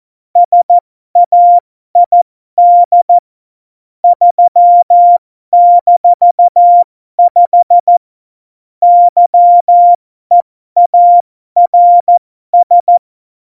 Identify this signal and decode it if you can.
Morse code, heard as a keyed tone, about 14 wpm, spaced as for 12 wpm, 700 Hz. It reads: SAID 3-5 YEARS
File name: Morsekurs / Witze